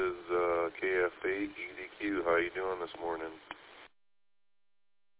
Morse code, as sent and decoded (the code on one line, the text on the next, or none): none